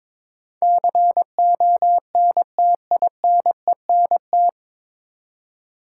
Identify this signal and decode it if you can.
Morse code, heard as a keyed tone, about 22 wpm, 700 Hz. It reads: CONTINENT
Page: Morse Code Ninja practice 22wpm